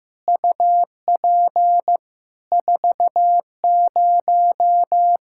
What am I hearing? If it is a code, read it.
Morse code, heard as a keyed tone, about 15 wpm, 700 Hz. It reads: UP 40
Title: K1USN SST QSO Practice 15wpm